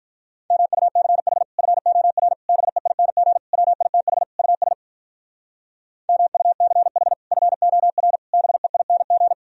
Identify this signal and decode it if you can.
Morse code, heard as a keyed tone, about 44 wpm, 700 Hz. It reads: MUCH FOR BEING WITH US MUCH FOR BEING
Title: Speed-Racing - Sets of 5 Words 35wpm